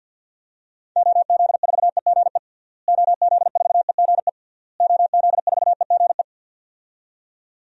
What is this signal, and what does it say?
Morse code, heard as a keyed tone, about 51 wpm, 700 Hz. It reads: OZ4EGE OZ4EGE OZ4EGE
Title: DXCC-100 Call Signs - 6 Characters - Repeated 3x - 50wpm